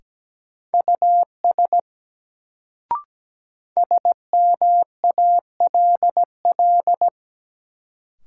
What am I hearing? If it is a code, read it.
Morse code, heard as a keyed tone, about 17 wpm, 700 Hz. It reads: US SMALL
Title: Sentences from Top 200 Words - New Word Drill - 17wpm